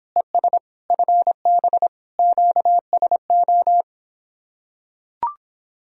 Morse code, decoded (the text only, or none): ES FB QSO